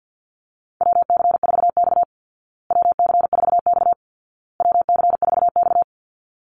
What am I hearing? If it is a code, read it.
Morse code, heard as a keyed tone, about 50 wpm, 700 Hz. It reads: WC4X WC4X WC4X
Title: Call Signs Repeated 2x - 50wpm